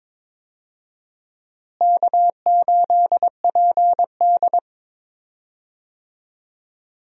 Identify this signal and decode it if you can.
Morse code, heard as a keyed tone, about 22 wpm, 700 Hz. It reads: K8PD
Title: USA Call Signs - 4 Characters - Repeated 3x - 22wpm